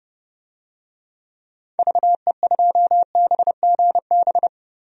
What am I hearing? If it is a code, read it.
Morse code, heard as a keyed tone, about 30 wpm, 700 Hz. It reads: VE2BGB
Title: DXCC-100 Call Signs - 6 Characters - 30wpm